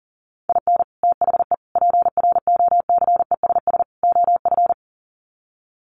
Morse code, heard as a keyed tone, about 40 wpm, 700 Hz. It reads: IN THE PROCESS OF
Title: Sets of 4 Words 40wpm